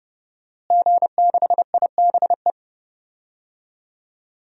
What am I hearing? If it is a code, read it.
Morse code, heard as a keyed tone, about 30 wpm, 700 Hz. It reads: G6IBE
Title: DXCC-100 Call Signs - 5 Characters - Repeated 2x - 30wpm